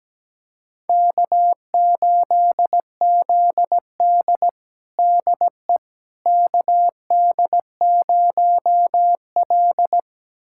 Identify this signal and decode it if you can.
Morse code, heard as a keyed tone, about 17 wpm, 700 Hz. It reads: K8ZD DE KD0L
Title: USA Callsign DE Callsign - 4 Characters - 17wpm